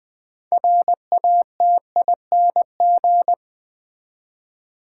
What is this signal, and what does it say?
Morse code, heard as a keyed tone, about 20 wpm, 700 Hz. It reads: RATING
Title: 6 Letter Words - First 3 Letters Given - 20wpm